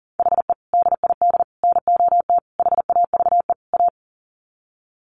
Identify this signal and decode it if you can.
Morse code, heard as a keyed tone, about 40 wpm, 700 Hz. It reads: HE DID NOT HAVE A
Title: Sets of 5 Words Encore 40wpm